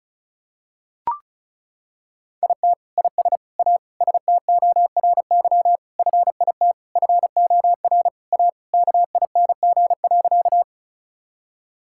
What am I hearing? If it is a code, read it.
Morse code, heard as a keyed tone, about 34 wpm, 700 Hz. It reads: IT IS A STORY FIT FOR A KING.
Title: Sentences from Top 1000 Words - Review Part 5 - 35wpm